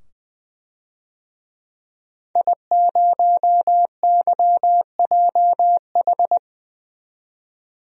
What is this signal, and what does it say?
Morse code, heard as a keyed tone, about 20 wpm, 700 Hz. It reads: I0YJH
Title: DXCC-100 Call Signs - 5 Characters - 20wpm